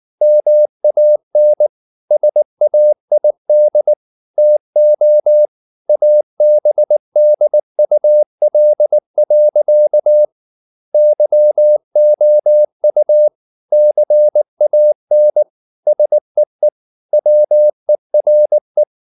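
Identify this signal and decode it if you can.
Morse code, heard as a keyed tone, about 19 wpm, 600 Hz. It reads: MAN SAID TO ABDUL. YOU CAN SEE WERE
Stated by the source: Morse code at 19 wpm from CW med Gnister, reading Ken Follett's novel Never in English.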